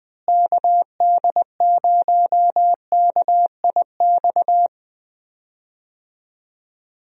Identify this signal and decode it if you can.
Morse code, heard as a keyed tone, about 20 wpm, 700 Hz. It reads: KD0KIX
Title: USA Call Signs - 6 Characters - Repeated 3x - 20wpm